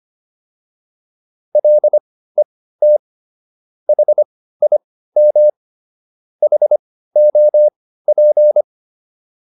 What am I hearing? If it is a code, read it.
Morse code, heard as a keyed tone, about 25 wpm, spaced as for 15 wpm, 600 Hz. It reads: LET HIM HOP